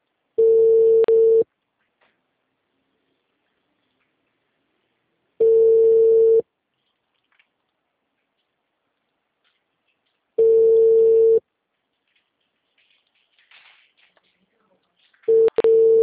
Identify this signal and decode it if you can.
Ringback tone (China, 450 Hz).
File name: out- -2005- - - .640